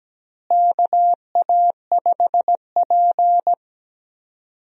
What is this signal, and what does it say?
Morse code, heard as a keyed tone, about 17 wpm, 700 Hz. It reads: KA5P
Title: USA Call Signs - 4 Characters - Repeated 2x - 17wpm